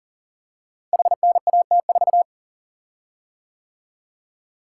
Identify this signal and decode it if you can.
Morse code, heard as a keyed tone, about 40 wpm, 700 Hz. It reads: HNAT4
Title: Code Groups - 5 Letter:Numbers - 40wpm